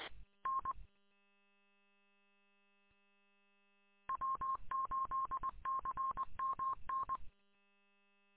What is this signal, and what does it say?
Morse code, heard as a keyed tone, about 20 wpm, 1070 Hz. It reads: N W8CMN
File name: w8cmn- -combined